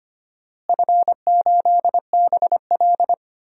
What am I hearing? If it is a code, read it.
Morse code, heard as a keyed tone, about 25 wpm, 700 Hz. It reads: F8BL
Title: DXCC-100 Call Signs - 4 Characters - Repeated 2x - 25wpm